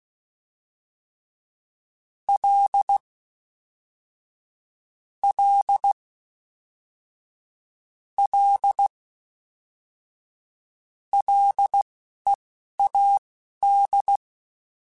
Morse code, heard as a keyed tone, about 16 wpm, spaced as for 11 wpm, 800 Hz. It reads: L L L LEAD